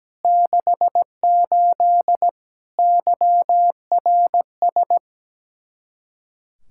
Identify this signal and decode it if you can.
Morse code, heard as a keyed tone, about 17 wpm, 700 Hz. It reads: 68 YRS
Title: Speed-Racing Common QSO Phrases Encore - 17wpm